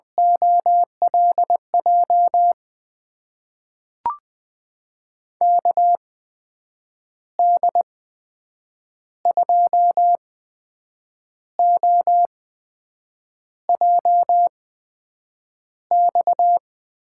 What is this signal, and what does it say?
Morse code, heard as a keyed tone, about 20 wpm, spaced as for 7 wpm, 700 Hz. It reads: OLJ KD2OJX
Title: USA Call Signs - 6 Characters - 8x Character Spacing - 20wpm